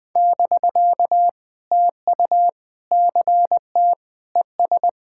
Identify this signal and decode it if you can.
Morse code, heard as a keyed tone, about 20 wpm, 700 Hz. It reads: <BK> TU CT ES
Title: POTA QSO Practice - 20wpm